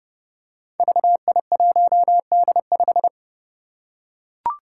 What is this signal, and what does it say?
Morse code, heard as a keyed tone, about 30 wpm, 700 Hz. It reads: VI1D5
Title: Five Challenging Letter-Number Combinations 30wpm